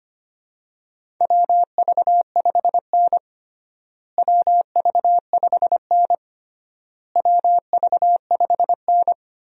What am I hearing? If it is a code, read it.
Morse code, heard as a keyed tone, about 25 wpm, 700 Hz. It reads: WV5N WV5N WV5N
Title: USA Call Signs - 4 Characters - Repeated 3x - 25wpm